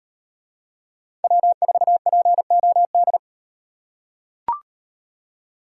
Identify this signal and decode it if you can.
Morse code, heard as a keyed tone, about 36 wpm, 700 Hz. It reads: W4POD